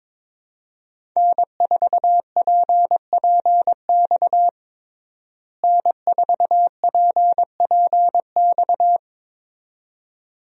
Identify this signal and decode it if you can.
Morse code, heard as a keyed tone, about 22 wpm, 700 Hz. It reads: N4PPX N4PPX
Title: USA Call Signs - 5 Characters - Repeated 2x - 22wpm